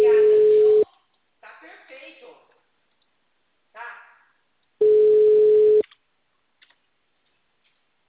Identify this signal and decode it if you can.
Ringback tone (Europe, 425 Hz), with speech.